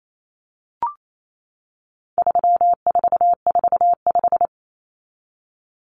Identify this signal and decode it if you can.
Morse code, heard as a keyed tone, about 28 wpm, 700 Hz. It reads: 3445
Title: Numbers - 4-Digit 28wpm